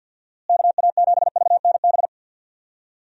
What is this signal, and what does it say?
Morse code, heard as a keyed tone, about 52 wpm, 700 Hz. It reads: KA7VNB
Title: USA Call Signs - 6 Characters - Repeated 3x - 50wpm